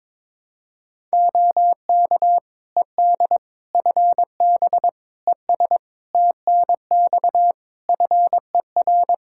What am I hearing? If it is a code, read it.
Morse code, heard as a keyed tone, about 21 wpm, 700 Hz. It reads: OK ED FB ES TNX FER
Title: K1USN SST QSO Practice 22wpm